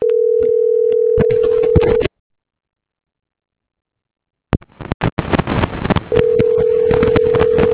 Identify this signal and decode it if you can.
Ringback tone (North America), 440 + 480 Hz.